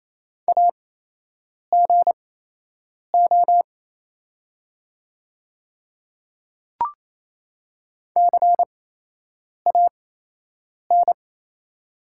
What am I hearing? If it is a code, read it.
Morse code, heard as a keyed tone, about 28 wpm, spaced as for 8 wpm, 700 Hz. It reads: AGO CAN
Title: 8x Character Spacing - Top 500 Words - 28wpm